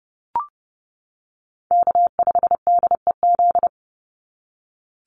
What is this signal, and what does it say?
Morse code, heard as a keyed tone, about 30 wpm, 700 Hz. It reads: K5DEZ
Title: NA SOTA Chaser Callsigns 30wpm